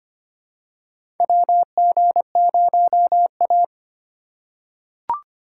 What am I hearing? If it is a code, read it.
Morse code, heard as a keyed tone, about 25 wpm, 700 Hz. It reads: WG0A
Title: USA Call Signs - 4 Characters - 4x Character Spacing - 25wpm